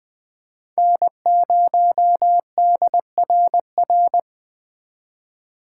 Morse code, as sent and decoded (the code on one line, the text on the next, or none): -. ----- -.. .-. .-.
N0DRR